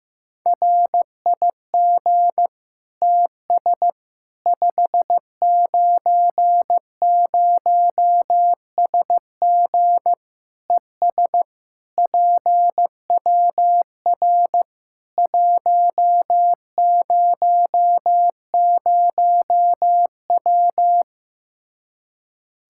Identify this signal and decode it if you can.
Morse code, heard as a keyed tone, about 15 wpm, 700 Hz. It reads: RIG TS 590SG ES PWR 100W
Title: Practice QSO - Exchange 2 - Info - 15wpm